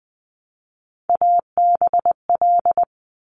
Morse code, heard as a keyed tone, about 19 wpm, 700 Hz. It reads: ABL
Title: Top Three Letter Combinations in Words 20wpm